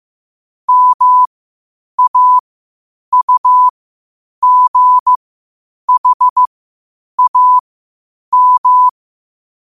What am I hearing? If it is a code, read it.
Morse code, heard as a keyed tone, about 15 wpm, spaced as for 8 wpm, 1000 Hz. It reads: MAUGHAM